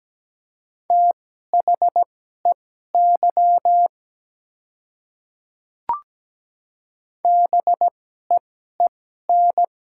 Morse code, heard as a keyed tone, about 17 wpm, spaced as for 12 wpm, 700 Hz. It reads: THEY BEEN
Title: 2x Character Spacing - Top 100 Words - 17wpm